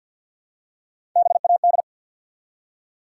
Morse code, heard as a keyed tone, about 47 wpm, 700 Hz. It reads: BAD
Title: Three Letter Words - 50wpm